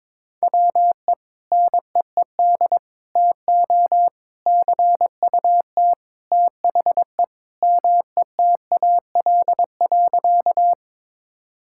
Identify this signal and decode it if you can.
Morse code, heard as a keyed tone, about 21 wpm, 700 Hz. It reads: WE NEED TO CUT THE METAL.
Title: Sentences from Top 1000 Words - Review Part 6 - 22wpm